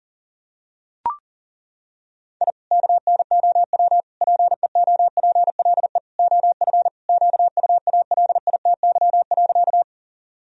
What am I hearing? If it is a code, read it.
Morse code, heard as a keyed tone, about 37 wpm, 700 Hz. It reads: I KNOW PEOPLE OF QUALITY.
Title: Sentences from Top 1000 Words - Review Part 4 - 40wpm